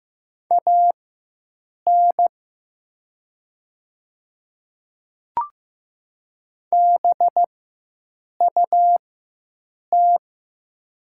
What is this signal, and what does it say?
Morse code, heard as a keyed tone, about 15 wpm, spaced as for 7 wpm, 700 Hz. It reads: AN BUT
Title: 4x Character Spacing - Top 100 Words - 15wpm